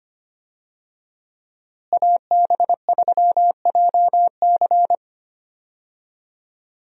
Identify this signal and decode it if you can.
Morse code, heard as a keyed tone, about 25 wpm, 700 Hz. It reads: AB3JC
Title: USA Call Signs - 5 Characters - Repeated 2x - 25wpm